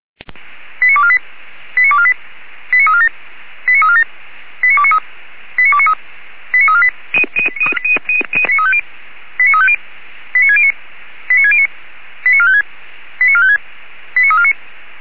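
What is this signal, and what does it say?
Five-tone selective call (ZVEI), heard as quick runs of five tones, tones of about 70 ms. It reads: 89248 89248 89347 89347 89293 89293 89348 89350 89609 89609 89456 89456 89349
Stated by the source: ZVEI five-tone selective call received on SDR, address 89360.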